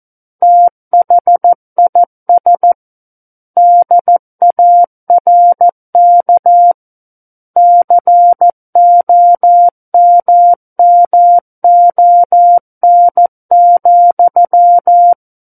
Morse code, heard as a keyed tone, about 14 wpm, 700 Hz. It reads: THIS DARK COMMON,
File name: Book One / 05-Chapter 5 - 14 WPM